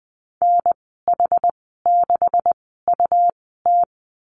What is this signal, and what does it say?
Morse code, heard as a keyed tone, about 20 wpm, spaced as for 15 wpm, 700 Hz. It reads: NH6UT
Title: 147 Lesson 24g - Letter y - Call Signs 20wpm